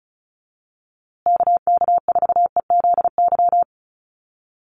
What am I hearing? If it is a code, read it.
Morse code, heard as a keyed tone, about 34 wpm, 700 Hz. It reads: KK4EZY
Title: USA Call Signs Repeated 2x 35wpm